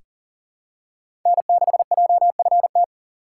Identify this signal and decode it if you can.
Morse code, heard as a keyed tone, about 40 wpm, 700 Hz. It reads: N6JFT
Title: USA Call Signs - 5 Characters - 3x Character Spacing - 40wpm